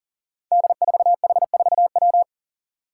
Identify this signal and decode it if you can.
Morse code, heard as a keyed tone, about 40 wpm, 690 Hz. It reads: D4H4W
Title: Five Challenging Letter-Number Combinations 40wpm